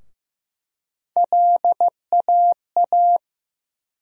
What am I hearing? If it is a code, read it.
Morse code, heard as a keyed tone, about 15 wpm, 700 Hz. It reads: LAA
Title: USA Call Sign Postfixes - 3 Letters 15wpm